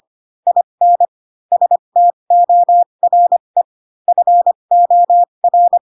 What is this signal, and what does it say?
Morse code, heard as a keyed tone, about 25 wpm, spaced as for 22 wpm, 700 Hz. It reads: IN STORE FOR
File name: Pinocchio / Pinocchio ch7 18 25 ik1hge